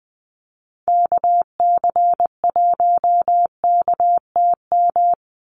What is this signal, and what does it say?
Morse code, heard as a keyed tone, about 20 wpm, 700 Hz. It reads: KC1KTM
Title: USA Call Signs 20wpm